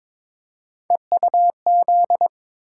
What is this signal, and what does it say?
Morse code, heard as a keyed tone, about 22 wpm, 700 Hz. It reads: EUZ
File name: practice / 22wpm / Three Letter-Number Combinations - 22wpm